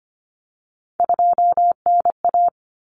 Morse code, heard as a keyed tone, about 25 wpm, 700 Hz. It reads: 2NA
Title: Common Cut Numbers - 3-Digit 25wpm